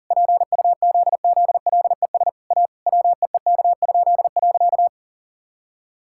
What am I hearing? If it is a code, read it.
Morse code, heard as a keyed tone, about 40 wpm, 700 Hz. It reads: PUZZLES A WEEK?.
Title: Pangram Challenge v2 - 40wpm